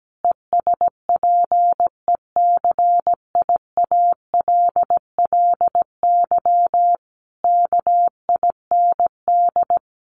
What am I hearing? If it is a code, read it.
Morse code, heard as a keyed tone, about 17 wpm, 700 Hz. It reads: ESPECIALLY KIND
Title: Sentences from Top 700 Words 17wpm